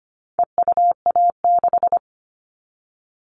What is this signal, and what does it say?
Morse code, heard as a keyed tone, about 25 wpm, 700 Hz. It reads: EUA6
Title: All Cut Numbers - 4-Digit 25wpm